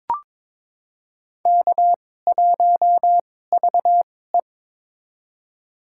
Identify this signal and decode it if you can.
Morse code, heard as a keyed tone, about 22 wpm, spaced as for 16 wpm, 700 Hz. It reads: K1VE
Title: USA Call Signs - 4 Characters - 2x Character Spacing - 22wpm